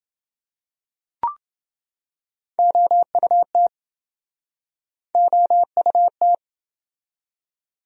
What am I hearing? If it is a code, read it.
Morse code, heard as a keyed tone, about 28 wpm, 700 Hz. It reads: OUT OUT